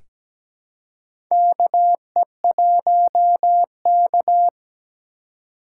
Morse code, heard as a keyed tone, about 17 wpm, 700 Hz. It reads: KE1K
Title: USA Call Signs - 4 Characters - 4x Character Spacing - 17wpm